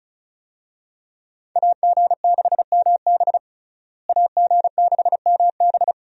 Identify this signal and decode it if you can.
Morse code, heard as a keyed tone, about 36 wpm, 700 Hz. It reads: AG6MB AG6MB
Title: USA Call Signs - 5 Characters - Repeated 2x - 35wpm